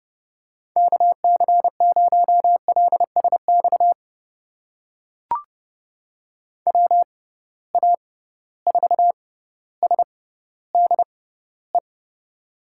Morse code, heard as a keyed tone, about 31 wpm, spaced as for 15 wpm, 700 Hz. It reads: KC0LSX WA4SDE
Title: USA Call Signs - 6 Characters - 6x Character Spacing - 30wpm